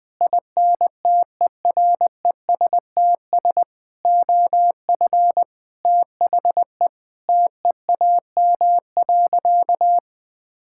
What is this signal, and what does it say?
Morse code, heard as a keyed tone, about 20 wpm, 700 Hz. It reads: INTERESTS OF THE TEAM.
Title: Sentences with 6-Word Phrases - Spoken Beginning - 20wpm